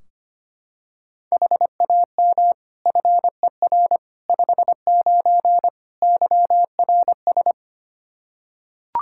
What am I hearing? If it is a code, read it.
Morse code, heard as a keyed tone, about 25 wpm, 700 Hz. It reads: HAM FER 59 YRS
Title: Practice QSO - Exchange 3 - More Info - 25wpm